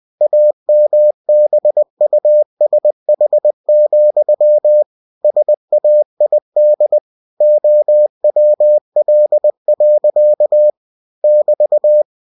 Morse code, heard as a keyed tone, about 20 wpm, 600 Hz. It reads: AMBUSH, SAID OWL. <BT>
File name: cw-books / winnie-the-pooh / winnie-20wpm-page-0082